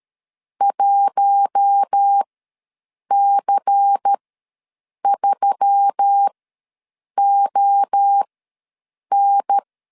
Morse code, heard as a keyed tone, about 13 wpm, spaced as for 7 wpm, 800 Hz. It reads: 1C3ON
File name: LEZIONI / 6 PKX0 / 6 PKX0 ws 12wpm 3448